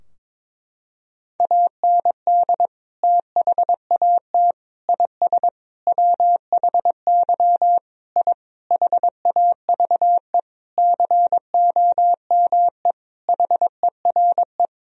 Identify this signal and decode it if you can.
Morse code, heard as a keyed tone, about 22 wpm, 700 Hz. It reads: AND THAT IS WHY I HAVE COME HERE
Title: Sentences from Top 1000 Words - Review Part 3 - 22wpm